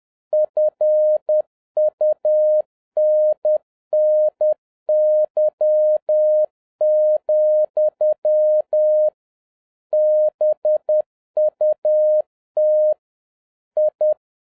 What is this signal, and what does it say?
Morse code, heard as a keyed tone, about 10 wpm, 610 Hz. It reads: FUNNY, BUT I